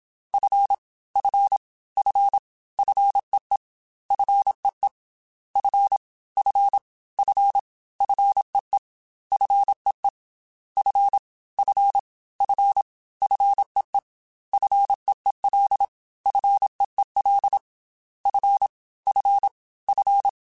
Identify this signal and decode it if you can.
Morse code, heard as a keyed tone, about 27 wpm, 800 Hz. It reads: F F F FEE FEE F F F FEE FEE F F F FEE FEEL FEEL F F F
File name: Single F mp3s / Single F Prefixes 3x2 Twice @25